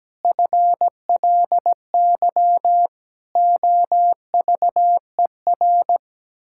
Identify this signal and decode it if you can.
Morse code, heard as a keyed tone, about 17 wpm, 700 Hz. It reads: FLY OVER
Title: Pangram Challenge v2 - 17wpm